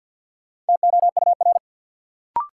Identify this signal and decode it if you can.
Morse code, heard as a keyed tone, about 47 wpm, 700 Hz. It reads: TOUR
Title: Sentences from Top 1400 Words - New Word Drill - 50wpm